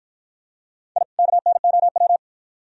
Morse code, heard as a keyed tone, about 48 wpm, 700 Hz. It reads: I KNOW